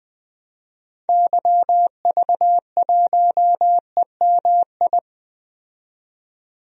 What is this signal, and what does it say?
Morse code, heard as a keyed tone, about 20 wpm, 700 Hz. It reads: YV1EMI
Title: DXCC-100 Call Signs - 6 Characters - 20wpm